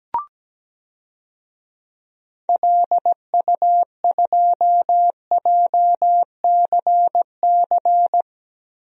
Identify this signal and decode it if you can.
Morse code, heard as a keyed tone, about 17 wpm, 700 Hz. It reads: LU2JCC
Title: DXCC-100 Call Signs - 6 Characters - Repeated 2x - 17wpm